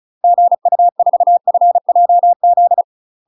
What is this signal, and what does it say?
Morse code, heard as a keyed tone, about 34 wpm, 700 Hz. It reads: GU4FJZ
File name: UK MASTER QSOs TRAINING / 35WPM / UK MASTER QSOs 0015